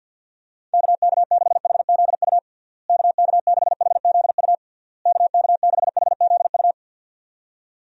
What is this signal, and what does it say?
Morse code, heard as a keyed tone, about 50 wpm, 700 Hz. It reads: KK6HZU KK6HZU KK6HZU
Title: USA Call Signs - 6 Characters - Repeated 3x - 50wpm